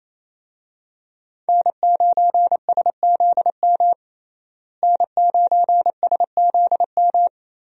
Morse code, heard as a keyed tone, about 28 wpm, 700 Hz. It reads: N9SZM N9SZM
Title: USA Call Signs - 5 Characters - Repeated 3x - 28wpm